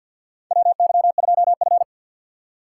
Morse code, heard as a keyed tone, about 49 wpm, 700 Hz. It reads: WY2F
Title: USA Call Signs - 4 Characters - 2x Character Spacing - 50wpm